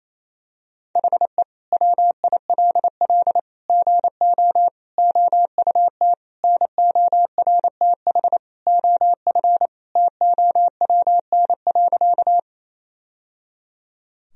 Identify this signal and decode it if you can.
Morse code, heard as a keyed tone, about 28 wpm, 700 Hz. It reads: HE WILL GO OUT NORTH OF TOWN.
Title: Sentences from Top 1000 Words - Review Part 1 - 28wpm